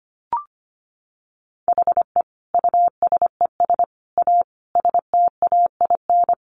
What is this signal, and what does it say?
Morse code, heard as a keyed tone, about 25 wpm, 700 Hz. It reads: HE USES A STAIN
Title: Additional Easy Phrases 25wpm